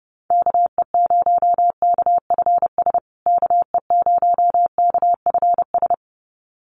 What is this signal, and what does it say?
Morse code, heard as a keyed tone, about 30 wpm, 700 Hz. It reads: KE0KFS KE0KFS
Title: USA Call Signs Repeated 2x 30wpm